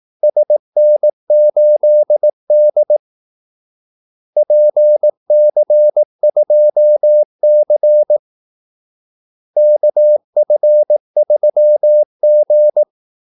Morse code, heard as a keyed tone, about 18 wpm, 600 Hz. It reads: SN8D PC2C KF3G
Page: Kaldesignaler 18 wpm